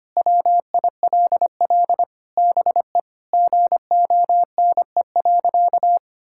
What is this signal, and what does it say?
Morse code, heard as a keyed tone, about 25 wpm, 700 Hz. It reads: WILL BE GONE.
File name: practice / 25wpm / Sentences with 4-Word Phrases - 25wpm